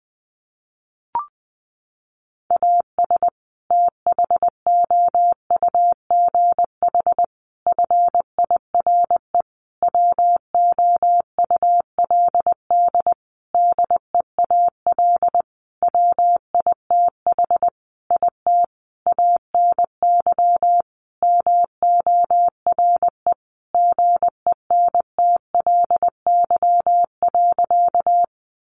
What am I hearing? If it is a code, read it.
Morse code, heard as a keyed tone, about 20 wpm, 700 Hz. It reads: AS THOUGH FIRE WOULD DEAL WITH IT ANY MORE GENTLY.
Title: Sentences from Top 1500 Words 20wpm